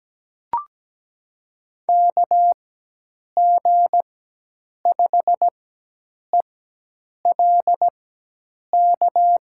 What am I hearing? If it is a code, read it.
Morse code, heard as a keyed tone, about 17 wpm, spaced as for 8 wpm, 700 Hz. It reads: KG5ELK
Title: USA Call Signs - 4x Character Spacing - 17wpm